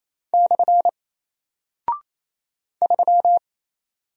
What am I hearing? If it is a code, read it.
Morse code, heard as a keyed tone, about 27 wpm, 700 Hz. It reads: / 3
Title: Single Character (Letter:Number:Punctuation:Prosign) Alt 28wpm